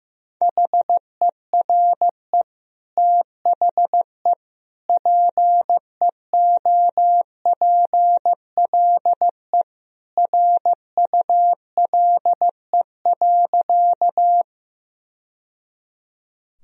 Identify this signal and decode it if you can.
Morse code, heard as a keyed tone, about 15 wpm, 700 Hz. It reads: HERE THE PEOPLE RULE.
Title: Sentences from Top 1000 Words - Review Part 2 - 15wpm